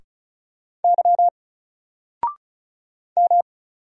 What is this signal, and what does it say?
Morse code, heard as a keyed tone, about 36 wpm, 700 Hz. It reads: Y M
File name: Single-Letter-Number-35wpm